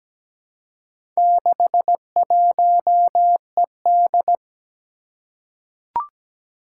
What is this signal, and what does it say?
Morse code, heard as a keyed tone, about 17 wpm, 700 Hz. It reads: 61ED
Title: Four Challenging Letter-Number Combinations 17wpm